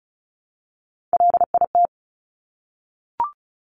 Morse code, heard as a keyed tone, about 36 wpm, 700 Hz. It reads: LIT
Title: Top Three Letter Combinations in Words 35wpm